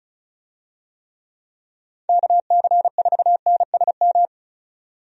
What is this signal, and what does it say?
Morse code, heard as a keyed tone, about 34 wpm, 700 Hz. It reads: KC4NSM
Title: USA Call Signs - 6 Characters - 35wpm